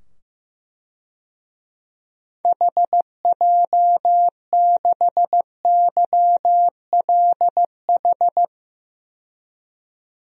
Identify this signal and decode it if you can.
Morse code, heard as a keyed tone, about 15 wpm, 700 Hz. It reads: HJ6YLH
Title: DXCC-100 Call Signs - 6 Characters - 15wpm